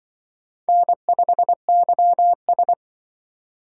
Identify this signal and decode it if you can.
Morse code, heard as a keyed tone, about 23 wpm, 700 Hz. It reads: N5YS
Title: Speed Racing - USA Call Signs 20wpm